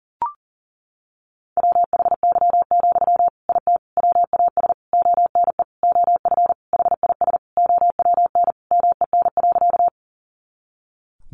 Morse code, heard as a keyed tone, about 40 wpm, 700 Hz. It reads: WHY, IT WAS ONE OF HIS OWN MEN.
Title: Sentences from Top 300 Words 40wpm